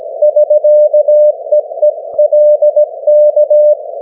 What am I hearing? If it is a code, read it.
Morse code, heard as a keyed tone, about 17 wpm, 600 Hz. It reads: <SK>EE LK
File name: GW3UEP / 16Dec12 9AM G3XIZ 501kHz